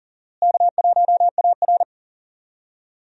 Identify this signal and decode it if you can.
Morse code, heard as a keyed tone, about 40 wpm, 700 Hz. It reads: K1AR
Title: USA Call Signs - 4 Characters - Repeated 3x - 40wpm